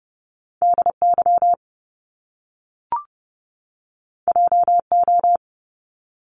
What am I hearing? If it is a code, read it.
Morse code, heard as a keyed tone, about 30 wpm, 700 Hz. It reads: DY JO
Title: Top Two Letter Combinations in Words 30wpm